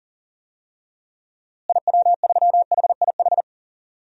Morse code, heard as a keyed tone, about 40 wpm, 700 Hz. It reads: IW3HIH
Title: DXCC-100 Call Signs - 6 Characters - Repeated 2x - 40wpm